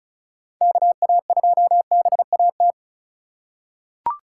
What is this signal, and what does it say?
Morse code, heard as a keyed tone, about 34 wpm, 700 Hz. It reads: KA2BAT
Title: USA Call Signs - 6 Characters - 3x Character Spacing - 35wpm